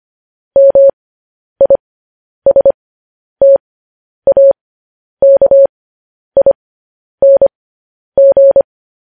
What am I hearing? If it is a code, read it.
Morse code, heard as a keyed tone, about 25 wpm, spaced as for 10 wpm, 550 Hz. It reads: MISTAKING
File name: podcast / News headlines-1x-25x10